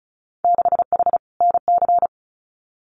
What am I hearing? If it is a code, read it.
Morse code, heard as a keyed tone, about 36 wpm, 700 Hz. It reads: 6H NC
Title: Winter Field Day Exchanges Encore 35wpm